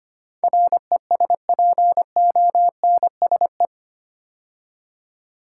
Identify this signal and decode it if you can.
Morse code, heard as a keyed tone, about 25 wpm, 700 Hz. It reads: RESPONSE
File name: 8 Letter Words - First 3 Letters Given - 25wpm